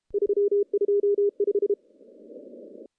Morse code, heard as a keyed tone, about 35 wpm, 410 Hz. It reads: 325
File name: contests / CW-SS-k5tr- -0100